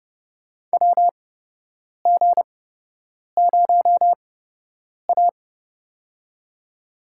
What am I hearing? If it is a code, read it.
Morse code, heard as a keyed tone, about 30 wpm, spaced as for 8 wpm, 700 Hz. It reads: WG0A
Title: USA Call Signs - 4 Characters - 8x Character Spacing - 30wpm